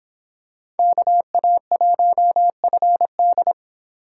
Morse code, heard as a keyed tone, about 26 wpm, 700 Hz. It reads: KA1FD